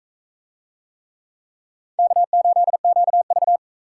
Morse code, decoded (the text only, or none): K8QV